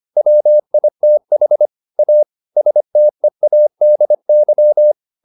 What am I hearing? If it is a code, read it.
Morse code, heard as a keyed tone, about 25 wpm, 600 Hz. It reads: WITH A STEADY
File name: the old man and the sea part 1 25wpm